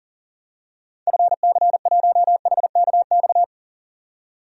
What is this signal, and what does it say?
Morse code, heard as a keyed tone, about 40 wpm, 700 Hz. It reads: FC1HKX
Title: DXCC-100 Call Signs - 6 Characters - 40wpm